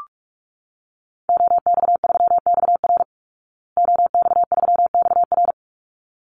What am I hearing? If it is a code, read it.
Morse code, heard as a keyed tone, about 44 wpm, 700 Hz. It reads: OX3XR OX3XR
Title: Call Signs Repeated 3x - 45wpm